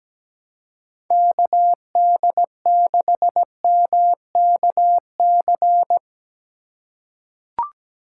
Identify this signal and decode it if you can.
Morse code, heard as a keyed tone, about 17 wpm, 700 Hz. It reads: KD6MKC
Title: USA Call Signs - 6 Characters - 3x Character Spacing - 17wpm